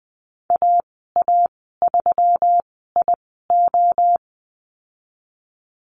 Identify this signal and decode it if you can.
Morse code, heard as a keyed tone, about 20 wpm, spaced as for 15 wpm, 700 Hz. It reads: AA3IO
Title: 167 Lesson 27g - Number 7 - Call Signs 20wpm